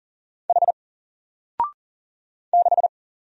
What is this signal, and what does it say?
Morse code, heard as a keyed tone, about 40 wpm, 700 Hz. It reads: H 6
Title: Single Character (Letter:Number:Punctuation:Prosign) 40wpm